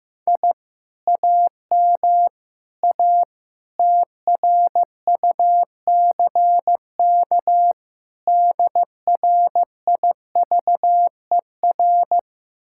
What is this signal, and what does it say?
Morse code, heard as a keyed tone, about 15 wpm, 700 Hz. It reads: I AM A TRUCK DRIVER
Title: Speed-Racing Common QSO Phrases Encore - 15wpm